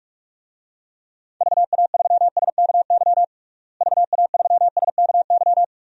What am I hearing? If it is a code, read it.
Morse code, heard as a keyed tone, about 46 wpm, 700 Hz. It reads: VA3SKY VA3SKY
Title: DXCC-100 Call Signs - 6 Characters - Repeated 3x - 45wpm